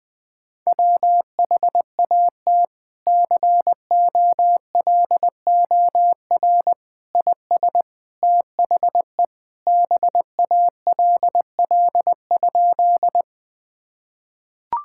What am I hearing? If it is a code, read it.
Morse code, heard as a keyed tone, about 20 wpm, 700 Hz. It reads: WHAT COLOR IS THE BALL?
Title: Sentences from Top 1000 Words - Review Part 6 - 20wpm